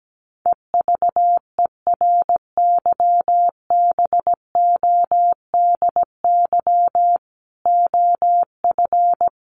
Morse code, heard as a keyed tone, about 17 wpm, 700 Hz. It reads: EVERYBODY OF
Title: Sentences from Top 1400 Words 17wpm